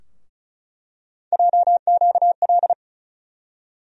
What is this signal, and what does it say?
Morse code, heard as a keyed tone, about 36 wpm, 700 Hz. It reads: JQL
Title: USA Call Sign Postfixes - 3 Letters 35wpm